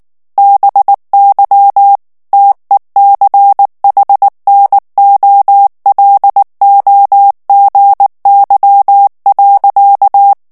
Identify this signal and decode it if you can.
Morse code, heard as a keyed tone, about 19 wpm, 800 Hz. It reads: BY TECHNOLOGY.
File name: ANGLAIS texte clair 3 vitesse 18 mots minute